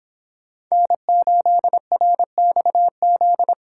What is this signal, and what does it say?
Morse code, heard as a keyed tone, about 26 wpm, 700 Hz. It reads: N8RXZ